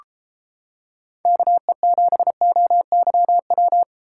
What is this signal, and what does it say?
Morse code, heard as a keyed tone, about 31 wpm, 700 Hz. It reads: KE7OYW